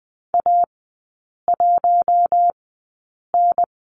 Morse code, heard as a keyed tone, about 20 wpm, spaced as for 8 wpm, 700 Hz. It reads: A1N
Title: 047 Lesson 9e - Number 4 - Three Characters 20wpm